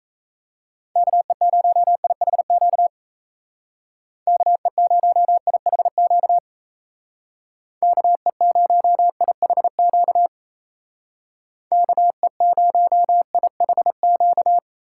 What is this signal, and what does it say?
Morse code, heard as a keyed tone, about 35 wpm, 700 Hz. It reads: KE0IHQ KE0IHQ KE0IHQ KE0IHQ
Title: Speed Racing - USA 2x3 Call Signs 28wpm